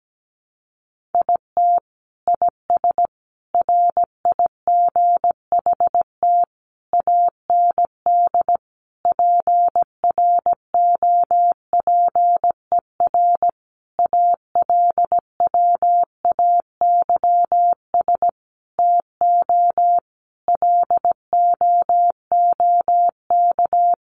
Morse code, heard as a keyed tone, about 17 wpm, 700 Hz. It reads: IT IS RIGHT AND PROPER ALWAYS TO LOOK
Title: Sentences from Top 1500 Words 17wpm